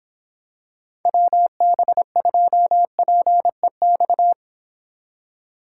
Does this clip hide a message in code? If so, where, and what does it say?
Yes: Morse code from 1.1 to 4.3 s, heard as a keyed tone, about 25 wpm, 700 Hz. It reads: WB2PEX